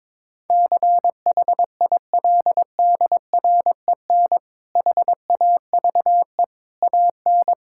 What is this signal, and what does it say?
Morse code, heard as a keyed tone, about 22 wpm, 700 Hz. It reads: CHILDREN HAVE AN
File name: Difficult Sentences with Hints - 22wpm